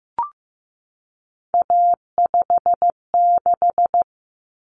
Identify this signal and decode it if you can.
Morse code, heard as a keyed tone, about 15 wpm, 700 Hz. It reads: A56
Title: Common Cut Numbers - 3-Digit 15wpm